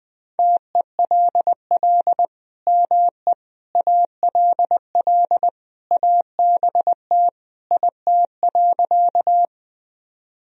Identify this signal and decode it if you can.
Morse code, heard as a keyed tone, about 20 wpm, 700 Hz. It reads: TELL ME ALL ABT IT.
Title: Sentences from Top 200 Words and Abbreviations 20wpm